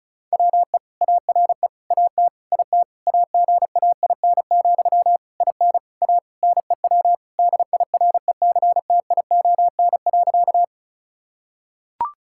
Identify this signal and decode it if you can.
Morse code, heard as a keyed tone, about 36 wpm, 700 Hz. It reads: WE ARE AT IT AGAIN, IN A NEW DIRECTION.
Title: Sentences from Top 1000 Words - Review Part 5 - 35wpm